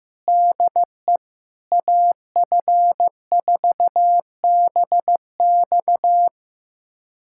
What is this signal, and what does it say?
Morse code, heard as a keyed tone, about 15 wpm, 700 Hz. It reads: DE AF4BX
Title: USA Callsign DE Callsign - 5 Characters - 15wpm